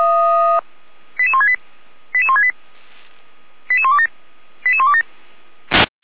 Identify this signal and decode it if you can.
Five-tone selective call (ZVEI), heard as quick runs of five tones, tones of about 70 ms. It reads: 88168 88168 88127 88127